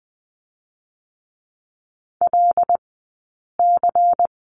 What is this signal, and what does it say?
Morse code, heard as a keyed tone, about 20 wpm, spaced as for 8 wpm, 700 Hz. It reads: LC
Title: 171 Lesson 28d - Number 9 - Two Characters 20wpm